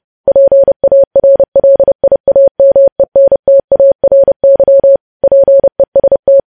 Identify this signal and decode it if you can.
Morse code, heard as a keyed tone, about 30 wpm, 550 Hz. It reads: PARLIAMENTARY PEST
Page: news headlines in morse code at 30 wpm